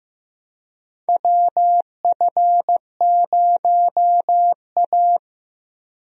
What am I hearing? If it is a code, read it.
Morse code, heard as a keyed tone, about 15 wpm, 700 Hz. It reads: WF0A
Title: USA Call Signs - 4 Characters - Repeated 2x - 15wpm